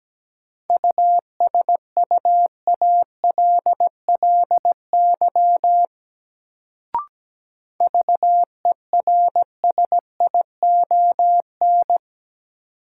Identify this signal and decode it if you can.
Morse code, heard as a keyed tone, about 17 wpm, 700 Hz. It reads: USUALLY VERSION
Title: Sentences from Top 1400 Words - New Word Drill Alt - 17wpm